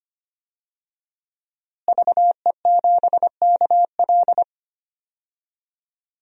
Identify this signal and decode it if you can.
Morse code, heard as a keyed tone, about 25 wpm, 700 Hz. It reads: VE7KL
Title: DXCC-100 Call Signs - 5 Characters - Repeated 3x - 25wpm